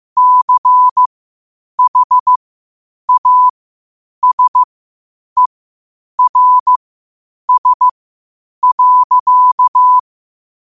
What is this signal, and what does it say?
Morse code, heard as a keyed tone, about 15 wpm, spaced as for 8 wpm, 1000 Hz. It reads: CHASERS.